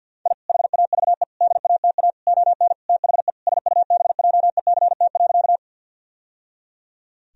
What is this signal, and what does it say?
Morse code, heard as a keyed tone, about 50 wpm, 700 Hz. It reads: I HAVE DATA ON THE SUBJECT.
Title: Sentences from Top 1000 Words - Review Part 3 - 50wpm